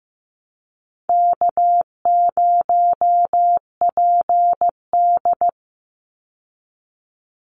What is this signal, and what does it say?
Morse code, heard as a keyed tone, about 15 wpm, 700 Hz. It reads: K0PD
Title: Easy Call Signs Repeated 2x - 15wpm